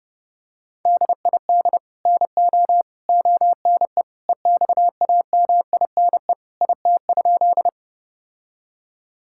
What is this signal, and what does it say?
Morse code, heard as a keyed tone, about 30 wpm, 700 Hz. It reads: DID NO ONE EXAMINE IT?
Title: Sentences from Top 1600 Words 30wpm